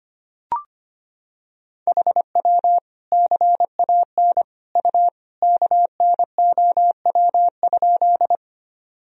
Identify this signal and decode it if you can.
Morse code, heard as a keyed tone, about 25 wpm, 700 Hz. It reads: HW CAN U KNOW?
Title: Sentences from Top 100 Words and Abbreviations 25wpm